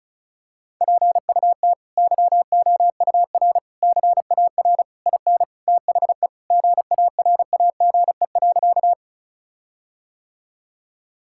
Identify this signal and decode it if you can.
Morse code, heard as a keyed tone, about 36 wpm, 700 Hz. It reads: PUT YOUR CAR IN THE GARAGE.
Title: Sentences from Top 1600 Words 35wpm